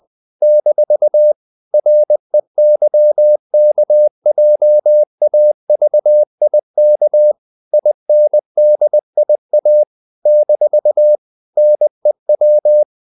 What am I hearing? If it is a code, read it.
Morse code, heard as a keyed tone, about 20 wpm, 600 Hz. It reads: - REYKJAVIK INDIA - NEW
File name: CW / Practice / NationalCapitals20